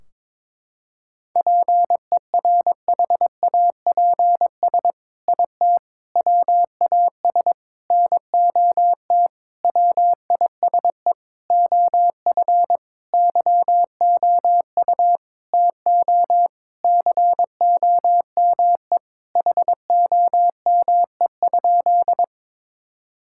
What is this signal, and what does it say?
Morse code, heard as a keyed tone, about 22 wpm, 700 Hz. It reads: PERHAPS IT WAS NOT WISE OF YOU TO COME HOME?
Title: Sentences from Top 1600 Words 22wpm